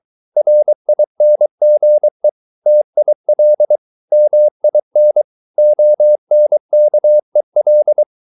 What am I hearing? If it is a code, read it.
Morse code, heard as a keyed tone, about 23 wpm, 600 Hz. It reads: RINGE TIL MIN ONKEL
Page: Ferie 23 wpm